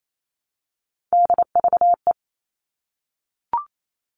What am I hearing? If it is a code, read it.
Morse code, heard as a keyed tone, about 28 wpm, 700 Hz. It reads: DVE
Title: All Cut Numbers - 3-Digit 28wpm